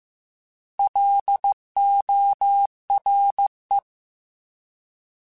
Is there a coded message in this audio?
Yes: Morse code from 0.8 to 3.8 s, heard as a keyed tone, about 15 wpm, 800 Hz. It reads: LORE